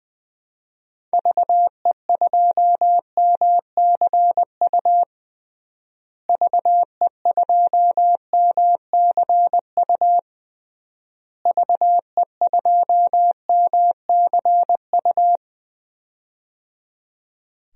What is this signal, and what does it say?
Morse code, heard as a keyed tone, about 20 wpm, 700 Hz. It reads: VE2MCU VE2MCU VE2MCU
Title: DXCC-100 Call Signs - 6 Characters - Repeated 3x - 20wpm